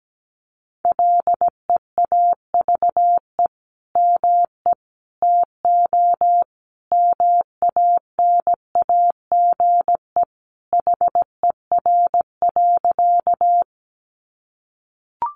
Morse code, heard as a keyed tone, about 17 wpm, 700 Hz. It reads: LEAVE ME TO MANAGE HER.
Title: Sentences from Top 1400 Words 17wpm